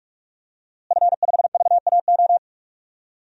Morse code, heard as a keyed tone, about 46 wpm, 700 Hz. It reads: F5VAO
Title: DXCC-100 Call Signs - 5 Characters - Repeated 3x - 45wpm